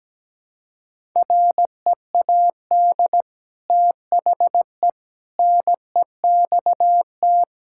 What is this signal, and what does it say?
Morse code, heard as a keyed tone, about 17 wpm, 700 Hz. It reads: READ THE NEXT
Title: Sentences from Top 1000 Words - Review Part 5 - 17wpm